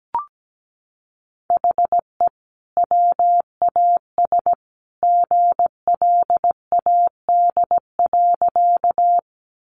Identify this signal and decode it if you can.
Morse code, heard as a keyed tone, about 17 wpm, 700 Hz. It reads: HE WAS GLAD.
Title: Sentences from Top 800 Words 17wpm